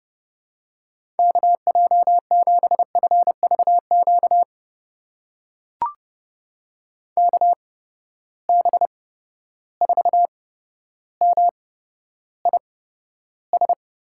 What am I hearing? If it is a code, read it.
Morse code, heard as a keyed tone, about 30 wpm, spaced as for 13 wpm, 700 Hz. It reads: KJ7FVQ KB4MIS
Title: USA Call Signs - 6 Characters - 8x Character Spacing - 30wpm